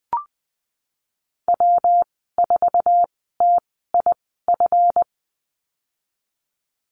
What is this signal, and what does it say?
Morse code, heard as a keyed tone, about 20 wpm, spaced as for 15 wpm, 700 Hz. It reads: W4TIF
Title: 179 Lesson 29g - Slash - Call Signs 20wpm